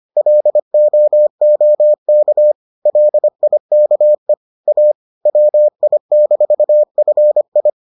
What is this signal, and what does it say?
Morse code, heard as a keyed tone, about 25 wpm, 600 Hz. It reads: LOOK LIKE A WI-FI